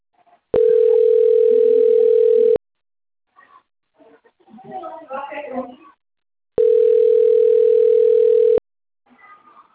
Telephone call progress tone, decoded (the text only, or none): ringback tone